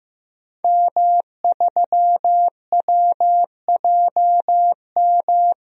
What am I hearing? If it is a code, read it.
Morse code, heard as a keyed tone, about 15 wpm, 700 Hz. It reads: M3WJM
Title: DXCC-100 Call Signs - 5 Characters - 15wpm